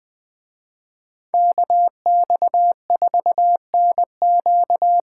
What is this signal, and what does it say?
Morse code, heard as a keyed tone, about 21 wpm, 700 Hz. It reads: KX4NQ
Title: Speed Racing - USA 2x2 Call Signs 17wpm